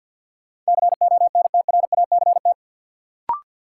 Morse code, heard as a keyed tone, about 52 wpm, 700 Hz. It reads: CONTRACT
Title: Sentences from Top 1400 Words - New Word Drill - 50wpm